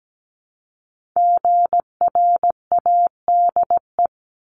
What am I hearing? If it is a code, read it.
Morse code, heard as a keyed tone, about 17 wpm, 700 Hz. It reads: GRADE
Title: Multi-letter Phonemes 17wpm